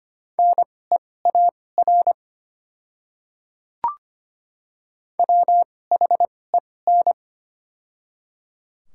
Morse code, heard as a keyed tone, about 25 wpm, spaced as for 18 wpm, 690 Hz. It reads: NEAR WHEN
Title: 2x Character Spacing - Top 200 Words - 25wpm